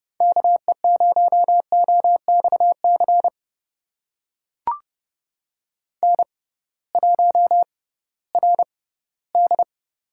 Morse code, heard as a keyed tone, about 32 wpm, spaced as for 18 wpm, 700 Hz. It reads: KE0OXC N1RD
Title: USA Call Signs - 6x Character Spacing - 30wpm